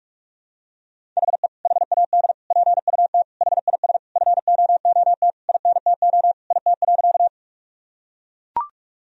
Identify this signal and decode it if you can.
Morse code, heard as a keyed tone, about 41 wpm, 700 Hz. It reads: HE HAD PUT HIS FOOT INTO IT.
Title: Sentences from Top 1000 Words - Review Part 3 - 45wpm